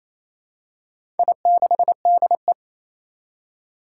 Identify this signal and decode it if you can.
Morse code, heard as a keyed tone, about 28 wpm, 700 Hz. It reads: I6DE
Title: Four Challenging Letter-Number Combinations 28wpm